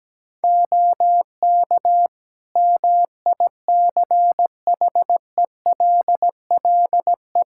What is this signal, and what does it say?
Morse code, heard as a keyed tone, about 17 wpm, 700 Hz. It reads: OK MICHELLE
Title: Practice QSO - Exchange 4 - Ending - 17wpm